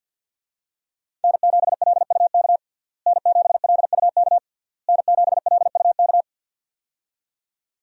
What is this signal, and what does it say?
Morse code, heard as a keyed tone, about 51 wpm, 700 Hz. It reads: N7LUK N7LUK N7LUK
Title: USA Call Signs - 5 Characters - Repeated 3x - 50wpm